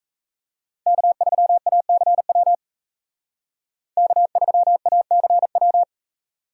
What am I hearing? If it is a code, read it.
Morse code, heard as a keyed tone, about 36 wpm, 700 Hz. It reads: K3ACW K3ACW